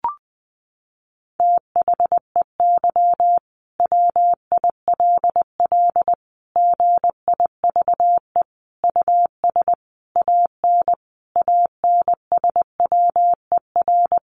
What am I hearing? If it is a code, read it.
Morse code, heard as a keyed tone, about 20 wpm, 700 Hz. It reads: THEY WILL GIVE US AN ANSWER
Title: Sentences from Top 600 Words 20wpm